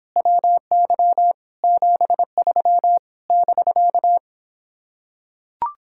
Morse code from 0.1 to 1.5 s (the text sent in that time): WY